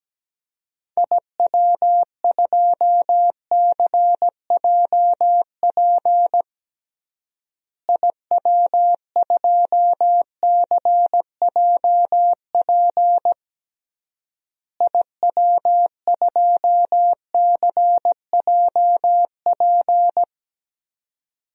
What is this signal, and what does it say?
Morse code, heard as a keyed tone, about 17 wpm, 700 Hz. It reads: IW2CJP IW2CJP IW2CJP
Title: DXCC-100 Call Signs - 6 Characters - Repeated 3x - 17wpm